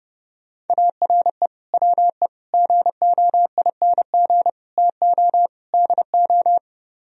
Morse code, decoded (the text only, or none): ARE WE GOING TO DO